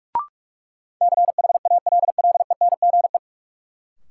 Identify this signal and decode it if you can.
Morse code, heard as a keyed tone, about 43 wpm, 700 Hz. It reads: CHALLENGE
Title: Sentences from Top 1000 Words - Review Part 7 - New Word Drill - 45wpm